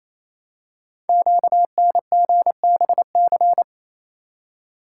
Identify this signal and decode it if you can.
Morse code, heard as a keyed tone, about 27 wpm, 700 Hz. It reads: QNGBC
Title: Code Groups - 5 Letters - 28wpm